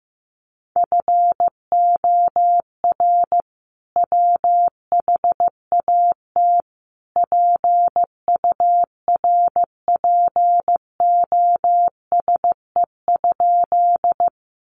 Morse code, heard as a keyed tone, about 15 wpm, 700 Hz. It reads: FOR WHAT PURPOSE?
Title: Sentences from Top 800 Words 15wpm